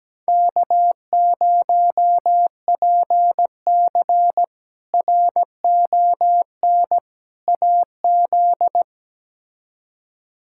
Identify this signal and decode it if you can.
Morse code, heard as a keyed tone, about 17 wpm, 700 Hz. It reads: K0PC RON AZ
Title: K1USN SST QSO Practice - 17wpm